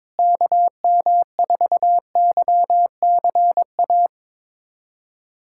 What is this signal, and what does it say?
Morse code, heard as a keyed tone, about 22 wpm, 700 Hz. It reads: KM4YCA
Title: USA Callsign DE Callsign - 6 Characters - 22wpm